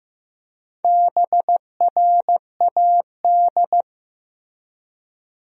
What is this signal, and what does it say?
Morse code, heard as a keyed tone, about 15 wpm, 700 Hz. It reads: BRAD
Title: USA Common Names - Max 5 Letters - 15wpm